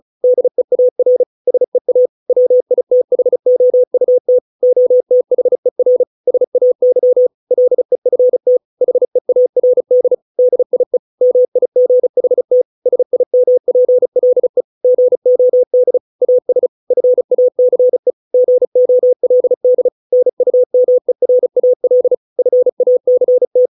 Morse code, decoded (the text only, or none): DEAR SEA WITHOUT OTHER SAY LEFT HEARD DIE MIGHT SIMPLE GOD AS FACE GOLD NUMERAL FACT